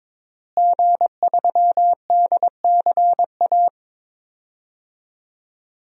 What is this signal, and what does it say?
Morse code, heard as a keyed tone, about 22 wpm, 700 Hz. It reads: G3DCA
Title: DXCC-100 Call Signs - 5 Characters - Repeated 3x - 22wpm